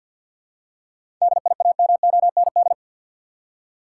Morse code, heard as a keyed tone, about 46 wpm, 700 Hz. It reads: DIAMOND
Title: Seven Letter Words - 50wpm